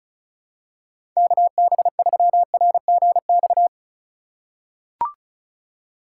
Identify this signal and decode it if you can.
Morse code, heard as a keyed tone, about 36 wpm, 700 Hz. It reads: KB3RGX
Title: USA Call Signs - 6 Characters - 3x Character Spacing - 35wpm